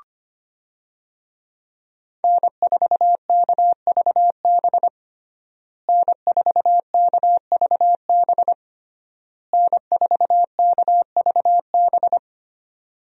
Morse code, heard as a keyed tone, about 25 wpm, 700 Hz. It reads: N4KVB N4KVB N4KVB
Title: USA Call Signs - 5 Characters - Repeated 3x - 25wpm